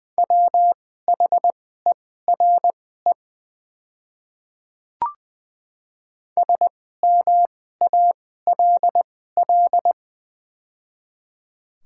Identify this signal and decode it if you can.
Morse code, heard as a keyed tone, about 20 wpm, spaced as for 14 wpm, 700 Hz. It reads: WHERE SMALL
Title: 2x Character Spacing - Top 200 Words - 20wpm